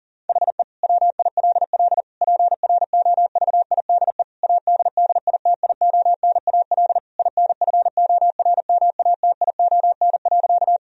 Morse code, heard as a keyed tone, about 40 wpm, 700 Hz. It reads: HE WILL PROVIDE ADDITIONAL INFORMATION.
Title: Sentences from Top 1000 Words - Review Part 4 - 40wpm